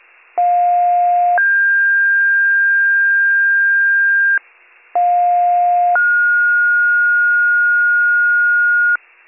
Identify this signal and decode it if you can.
Two-tone page, heard as a one-second tone, then a three-second tone, about 710 then 1600 Hz, about 710 then 1430 Hz, with noise. It is heard from 0.4 to 9.0 s.